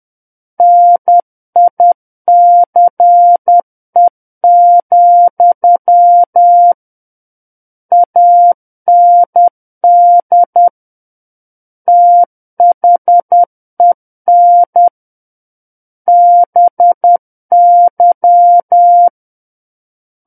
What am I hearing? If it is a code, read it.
Morse code, heard as a keyed tone, about 10 wpm, 700 Hz. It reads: NICE, AND THEN BY
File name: Book One / 01-Chapter 1 - 10 WPM